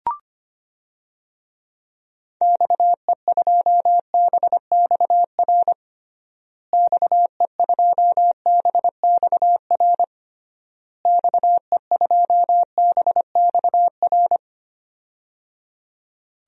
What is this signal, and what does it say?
Morse code, heard as a keyed tone, about 25 wpm, 700 Hz. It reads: XE2BXR XE2BXR XE2BXR
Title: DXCC-100 Call Signs - 6 Characters - Repeated 3x - 25wpm